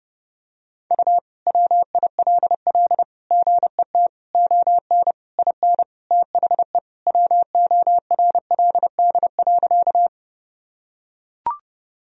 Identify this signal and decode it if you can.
Morse code, heard as a keyed tone, about 32 wpm, 700 Hz. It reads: U WILL GET ON IN THE WORLD.
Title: Sentences from Top 200 Words and Abbreviations 30wpm